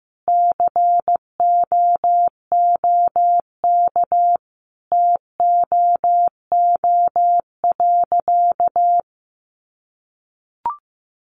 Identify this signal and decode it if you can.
Morse code, heard as a keyed tone, about 15 wpm, 700 Hz. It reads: COOK TOO.
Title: Sentences from Top 700 Words 15wpm